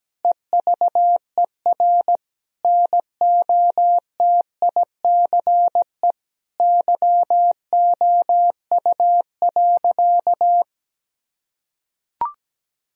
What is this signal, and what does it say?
Morse code, heard as a keyed tone, about 17 wpm, 700 Hz. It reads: EVER NOTICE YOU.
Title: Sentences from Top 1000 Words - Review Part 4 - 17wpm